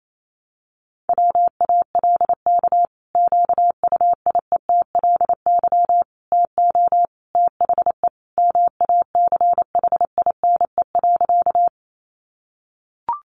Morse code, heard as a keyed tone, about 28 wpm, 700 Hz. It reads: WALK QUIETLY TO THE MACHINE.
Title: A Sound Of Thunder 28wpm